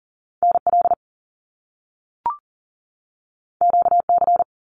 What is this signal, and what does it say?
Morse code, heard as a keyed tone, about 42 wpm, 700 Hz. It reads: NL QC
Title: Canadian Provinces and Territories 40wpm